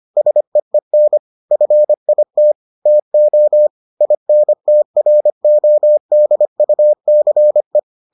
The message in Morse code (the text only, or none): SEEN FIT TO INTRODUCE